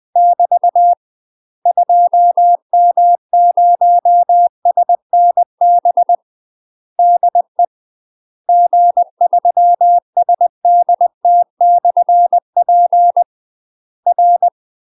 Morse code, heard as a keyed tone, about 20 wpm, 700 Hz. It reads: <BT> 2M0SNB DE G3SDT/P R